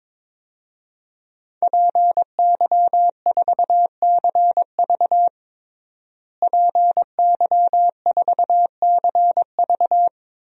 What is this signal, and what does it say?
Morse code, heard as a keyed tone, about 22 wpm, 700 Hz. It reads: PY4CV PY4CV
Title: DXCC-100 Call Signs - 5 Characters - Repeated 2x - 22wpm